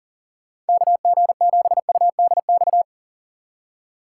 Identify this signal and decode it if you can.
Morse code, heard as a keyed tone, about 52 wpm, 700 Hz. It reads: KG7UDX